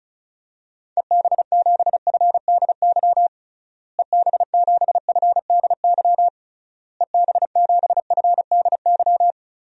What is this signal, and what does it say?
Morse code, heard as a keyed tone, about 36 wpm, 700 Hz. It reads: EB7FDY EB7FDY EB7FDY
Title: DXCC-100 Call Signs - 6 Characters - Repeated 3x - 35wpm